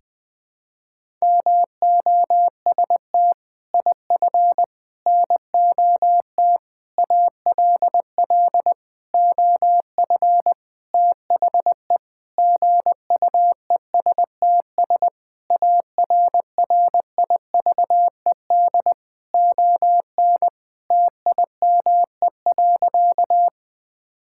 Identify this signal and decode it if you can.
Morse code, heard as a keyed tone, about 20 wpm, 700 Hz. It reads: MOST IF NOT ALL OF THE GUESTS ARRIVED ON TIME.
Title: Sentences with 6-Word Phrases - 20wpm